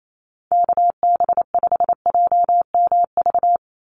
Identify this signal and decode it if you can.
Morse code, heard as a keyed tone, about 28 wpm, 700 Hz. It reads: KB5JMV
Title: USA Call Signs 28wpm